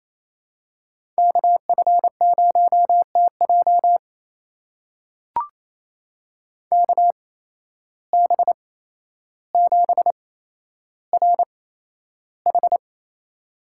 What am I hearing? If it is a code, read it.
Morse code, heard as a keyed tone, about 28 wpm, spaced as for 12 wpm, 700 Hz. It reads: KF0TJ KB7RH
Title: USA Call Signs - 5 Characters - 8x Character Spacing - 28wpm